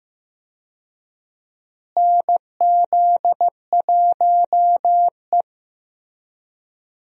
Morse code, heard as a keyed tone, about 15 wpm, 700 Hz. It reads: NZ1E
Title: USA Call Signs - 4 Characters - Repeated 2x - 15wpm